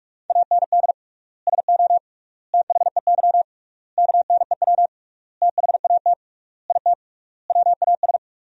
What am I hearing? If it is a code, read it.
Morse code, heard as a keyed tone, about 46 wpm, 700 Hz. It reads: AND SO THEY KNEW THAT IT WAS
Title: 3x Word-Spacing - Sentences from Top 400 Words - 45wpm